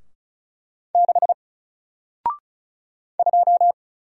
Morse code, heard as a keyed tone, about 35 wpm, 700 Hz. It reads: 6 2
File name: Single-Letter-Number-35wpm